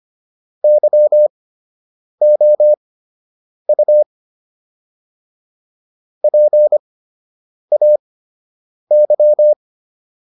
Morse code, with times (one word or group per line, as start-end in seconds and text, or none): 0.6-4.0 YOU
6.2-9.5 PAY